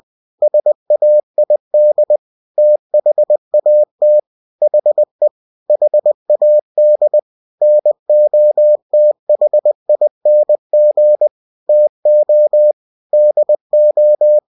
Morse code, heard as a keyed tone, about 20 wpm, 600 Hz. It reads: SAID THAT HE HAD NOTHING TO DO
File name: cw-books / winnie-the-pooh / winnie-20wpm-page-0022